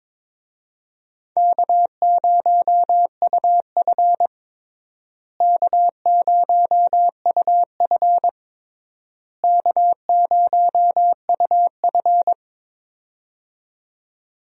Morse code, heard as a keyed tone, about 22 wpm, 700 Hz. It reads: K0UF K0UF K0UF
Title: USA Call Signs - 4 Characters - Repeated 3x - 22wpm